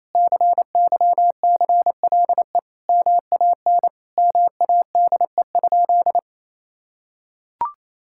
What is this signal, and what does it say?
Morse code, heard as a keyed tone, about 28 wpm, 700 Hz. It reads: CYCLE MAN MADE?
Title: Sentences from Top 1000 Words - Review Part 6 - 28wpm